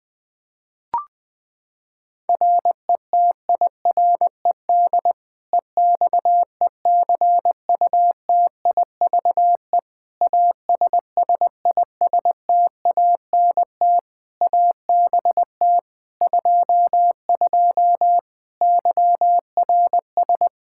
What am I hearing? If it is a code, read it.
Morse code, heard as a keyed tone, about 20 wpm, 700 Hz. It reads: RETIRED EXECUTIVE ASSISTANT ABT 22 YRS
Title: Practice QSO - Exchange 3 - More Info - 20wpm